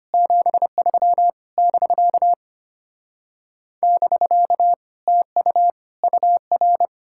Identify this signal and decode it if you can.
Morse code, heard as a keyed tone, about 33 wpm, 700 Hz. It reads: 73 <BK> <BK> TU UR